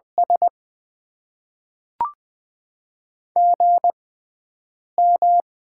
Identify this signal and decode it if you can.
Morse code, heard as a keyed tone, about 21 wpm, spaced as for 4 wpm, 700 Hz. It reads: SGM
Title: Top 20 QSO Elements - 6x Character Spacing - 20wpm